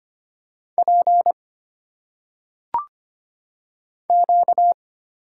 Morse code, heard as a keyed tone, about 25 wpm, 700 Hz. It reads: P Q
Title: Single Letters - Phonetic Pronunciation 25wpm